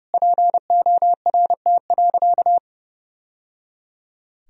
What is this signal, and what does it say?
Morse code, heard as a keyed tone, about 29 wpm, 700 Hz. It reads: PORT.
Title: Sentences from Top 1000 Words - Review Part 1 - 30wpm